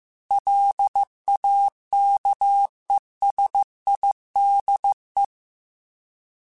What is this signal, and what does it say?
Morse code, heard as a keyed tone, about 15 wpm, 800 Hz. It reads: LAKESIDE